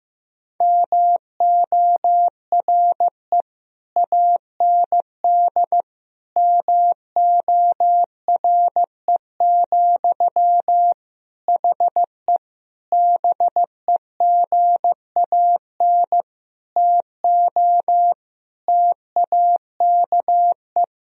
Morse code, heard as a keyed tone, about 15 wpm, 700 Hz. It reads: MORE AND MORE, HE BEGAN TO TAKE
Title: Sentences from Top 1000 Words - Review Part 3 - 15wpm